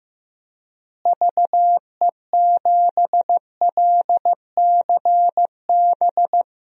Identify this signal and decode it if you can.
Morse code, heard as a keyed tone, about 15 wpm, 700 Hz. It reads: VE7LCB
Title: DXCC-100 Call Signs - 6 Characters - Repeated 3x - 15wpm